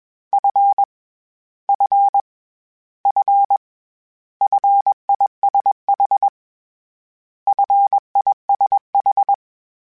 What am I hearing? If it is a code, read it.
Morse code, heard as a keyed tone, about 21 wpm, 800 Hz. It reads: F F F FISH FISH
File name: Short Runs / Short Runs-mp3s / SinglePrefixes @20